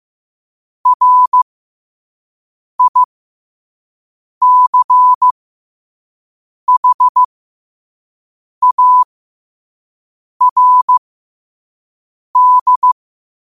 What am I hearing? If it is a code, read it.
Morse code, heard as a keyed tone, about 15 wpm, spaced as for 5 wpm, 1000 Hz. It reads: RICHARD
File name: qotdfiles / 5wpm